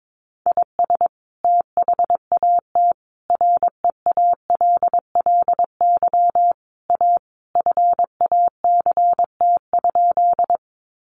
Morse code, heard as a keyed tone, about 22 wpm, 700 Hz. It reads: IS THAT REALLY A FACT?
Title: Sentences from Top 600 Words 22wpm